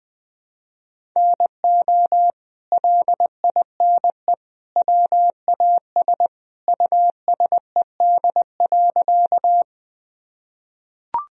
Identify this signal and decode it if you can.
Morse code, heard as a keyed tone, about 21 wpm, 700 Hz. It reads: NO LINE WAS USED.
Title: Sentences from Top 1000 Words - Review Part 1 - 20wpm